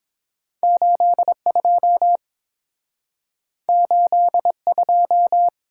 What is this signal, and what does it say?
Morse code, heard as a keyed tone, about 30 wpm, 700 Hz. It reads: 82 82